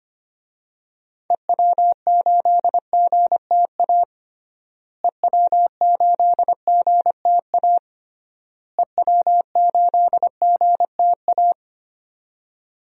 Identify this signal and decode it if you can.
Morse code, heard as a keyed tone, about 25 wpm, 700 Hz. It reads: EW8GTA EW8GTA EW8GTA
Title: DXCC-100 Call Signs - 6 Characters - Repeated 3x - 25wpm